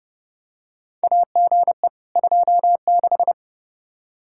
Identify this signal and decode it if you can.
Morse code, heard as a keyed tone, about 32 wpm, 700 Hz. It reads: AGE 26
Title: Practice QSO - Exchange 3 - More Info - 30wpm